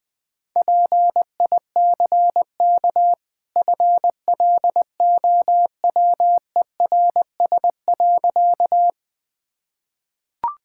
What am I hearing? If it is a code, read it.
Morse code, heard as a keyed tone, about 20 wpm, 700 Hz. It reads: PICK FLOWERS.
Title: Easy Sentences with Hints - 20wpm